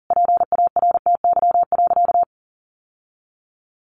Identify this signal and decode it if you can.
Morse code, heard as a keyed tone, about 40 wpm, 700 Hz. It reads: PARTY.
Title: Sentences from Top 1500 Words 40wpm